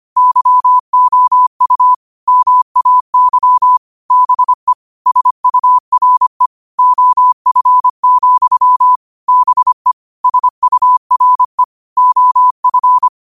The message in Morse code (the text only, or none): YOU MAY BE SURE OF, BE SURE OF